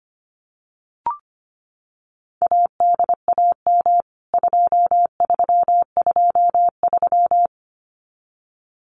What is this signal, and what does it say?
Morse code, heard as a keyed tone, about 25 wpm, 700 Hz. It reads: ADAM 2323
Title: CWT Exchange 25wpm